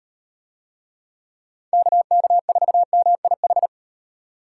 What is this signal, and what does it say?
Morse code, heard as a keyed tone, about 38 wpm, 700 Hz. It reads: KK4MIH